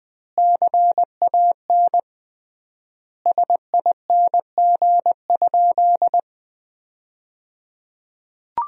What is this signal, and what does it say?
Morse code, heard as a keyed tone, about 21 wpm, 700 Hz. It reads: CAN SING?
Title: 3x Word-Spacing - Sentences from Top 400 Words - 20wpm